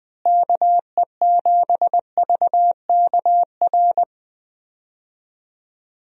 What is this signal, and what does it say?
Morse code, heard as a keyed tone, about 20 wpm, 700 Hz. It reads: KE7VKR
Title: USA Call Signs - 6 Characters - Repeated 3x - 20wpm